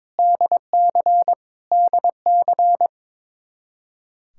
Morse code, heard as a keyed tone, about 21 wpm, 700 Hz. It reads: DC DC
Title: Practice QSO - Exchange 1 - Beginning - 22wpm